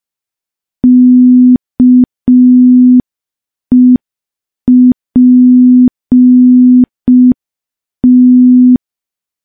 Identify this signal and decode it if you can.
Morse code, heard as a keyed tone, about 5 wpm, 250 Hz. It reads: KEPT
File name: morse / morse-training-05WPM